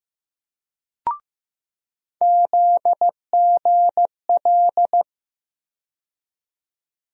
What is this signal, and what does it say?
Morse code, heard as a keyed tone, about 15 wpm, 700 Hz. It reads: ZGL
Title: USA Call Sign Postfixes - 3 Letters 15wpm